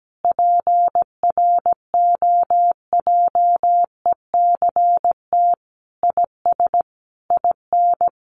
Morse code, heard as a keyed tone, about 17 wpm, 700 Hz. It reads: PROJECT IS IN
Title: Sentences from Top 1400 Words 17wpm